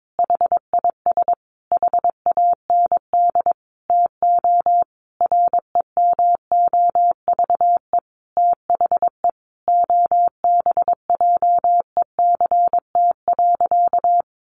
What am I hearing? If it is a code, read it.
Morse code, heard as a keyed tone, about 22 wpm, 700 Hz. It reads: HIS HAND TO REMOVE THE OBJECT.
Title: Sentences from Top 800 Words 22wpm